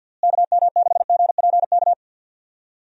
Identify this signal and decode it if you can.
Morse code, heard as a keyed tone, about 48 wpm, 700 Hz. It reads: KM6GPK
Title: USA Call Signs - 6 Characters - Repeated 2x - 50wpm